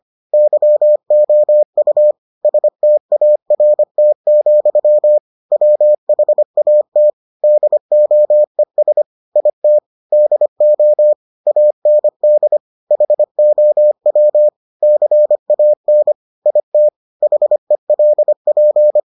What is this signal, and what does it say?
Morse code, heard as a keyed tone, about 25 wpm, 600 Hz. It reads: YOU START, WHAT DOES IT DO AND HOW CAN IT HELP